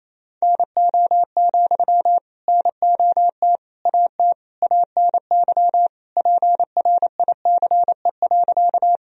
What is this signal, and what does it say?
Morse code, heard as a keyed tone, about 27 wpm, 700 Hz. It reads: NO, NOT AT ANY PRICE.
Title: Sentences from Top 1000 Words - Review Part 4 - 28wpm